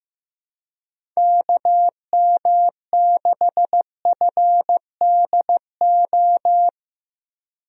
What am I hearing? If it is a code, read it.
Morse code, heard as a keyed tone, about 16 wpm, 700 Hz. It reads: KM6FDO
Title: USA Call Signs - 6 Characters - Repeated 2x - 15wpm